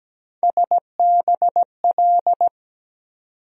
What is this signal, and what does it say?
Morse code, heard as a keyed tone, about 17 wpm, 700 Hz. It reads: SBL
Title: USA Call Sign Postfixes - 3 Letters 17wpm